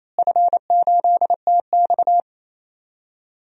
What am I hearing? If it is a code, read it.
Morse code, heard as a keyed tone, about 27 wpm, 700 Hz. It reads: F8TX
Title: DXCC-100 Call Signs - 4 Characters - Repeated 2x - 28wpm